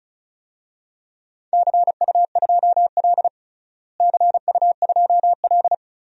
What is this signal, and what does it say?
Morse code, heard as a keyed tone, about 36 wpm, 690 Hz. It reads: CU2L CU2L
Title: DXCC-100 Call Signs - 4 Characters - Repeated 3x - 35wpm